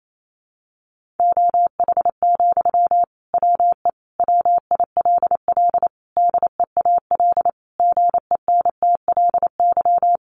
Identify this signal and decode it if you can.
Morse code, heard as a keyed tone, about 28 wpm, 700 Hz. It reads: OH, WE WILL DEAL GENTLY
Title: Sentences from Top 1500 Words 28wpm